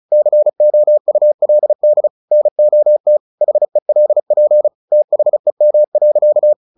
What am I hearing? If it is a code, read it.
Morse code, heard as a keyed tone, about 34 wpm, 600 Hz. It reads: COULD NOT HELP THEM.
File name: the old man and the sea / the old man and the sea 1 35wpm